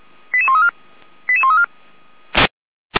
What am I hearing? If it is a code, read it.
Five-tone selective call (ZVEI), heard as quick runs of five tones, tones of about 70 ms. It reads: 88134 88134
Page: /455/SDR/ZVEI/88531/2024.09.23/